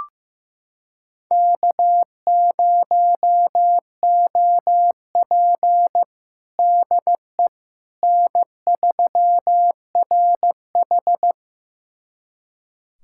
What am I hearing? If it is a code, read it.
Morse code, heard as a keyed tone, about 15 wpm, 700 Hz. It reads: K0OP DE N3RH
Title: USA Callsign DE Callsign - 4 Characters - 15wpm